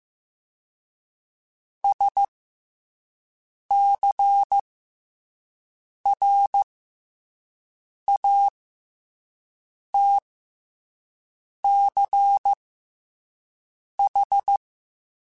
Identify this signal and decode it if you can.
Morse code, heard as a keyed tone, about 15 wpm, 800 Hz. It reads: S C R A T C H
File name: Special INGs-mp3s / Special INGs-mp3s / SlowToFaster ing @10